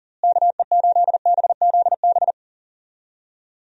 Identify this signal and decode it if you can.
Morse code, heard as a keyed tone, about 40 wpm, 700 Hz. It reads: KE8BZB
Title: USA Call Signs - 6 Characters - Repeated 3x - 40wpm